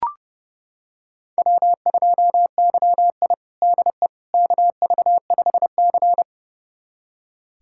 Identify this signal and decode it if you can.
Morse code, heard as a keyed tone, about 30 wpm, 700 Hz. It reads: W2YI DE KV5C
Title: USA Callsign DE Callsign - 4 Characters - 30wpm